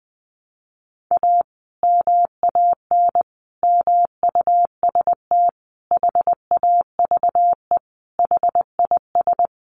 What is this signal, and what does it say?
Morse code, heard as a keyed tone, about 20 wpm, 700 Hz. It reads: A MAN MUST HAVE HIS
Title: Sentences from Top 1200 Words 20wpm